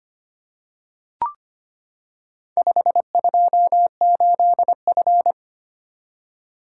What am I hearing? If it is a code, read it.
Morse code, heard as a keyed tone, about 25 wpm, 700 Hz. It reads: 528F
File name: practice / 25wpm / Four Letter-Number Combinations - 25wpm